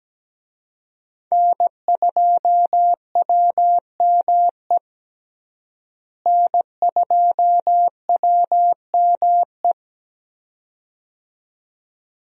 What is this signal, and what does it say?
Morse code, heard as a keyed tone, about 17 wpm, 700 Hz. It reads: N2WME N2WME
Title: USA Call Signs - 5 Characters - Repeated 2x - 17wpm